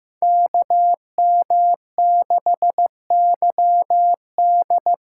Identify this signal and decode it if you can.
Morse code, heard as a keyed tone, about 15 wpm, 700 Hz. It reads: KM6YD
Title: USA Call Signs - 6x Character Spacing - 15wpm